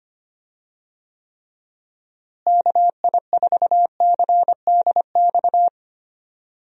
Morse code, heard as a keyed tone, about 25 wpm, 700 Hz. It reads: KI4CDX
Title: USA Call Signs - 6 Characters - Repeated 2x - 25wpm